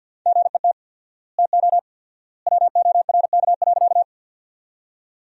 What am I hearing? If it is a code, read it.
Morse code, heard as a keyed tone, about 50 wpm, 700 Hz. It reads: GET TO WORK.
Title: 4x Word-Spacing - Sentences from Top 200 Words - 50wpm